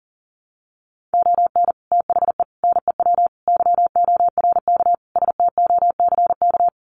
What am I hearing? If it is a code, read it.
Morse code, heard as a keyed tone, about 40 wpm, 700 Hz. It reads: ON THE NEW YORK STOCK
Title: Sets of 5 Words Encore 40wpm